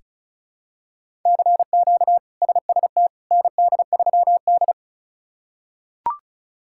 Morse code, heard as a keyed tone, about 36 wpm, 700 Hz. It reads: CQ SST ND3D
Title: K1USN SST QSO Practice - 35wpm